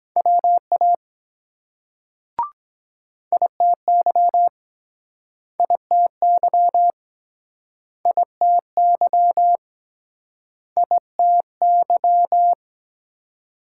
Morse code, heard as a keyed tone, about 22 wpm, 700 Hz. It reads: WA ITY ITY ITY ITY